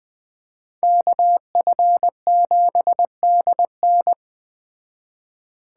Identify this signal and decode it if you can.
Morse code, heard as a keyed tone, about 20 wpm, 700 Hz. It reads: KF7DN
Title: Speed Racing - USA 2x2 Call Signs 17wpm